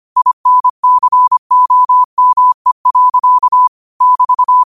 Morse code, heard as a keyed tone, about 25 wpm, 1000 Hz. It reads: INCOME. <BT>
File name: qotdfiles / 25wpm